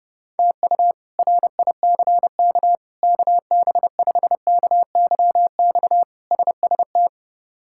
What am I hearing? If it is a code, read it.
Morse code, heard as a keyed tone, about 30 wpm, 700 Hz. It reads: TU RICK KB5KYX SST
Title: K1USN SST QSO Practice - 30wpm